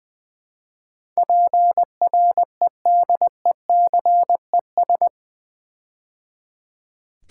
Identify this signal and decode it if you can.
Morse code, heard as a keyed tone, about 20 wpm, 700 Hz. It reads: PREDECES
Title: Build A Word - 4 Letters At A Time - 11-Letter Words - 20wpm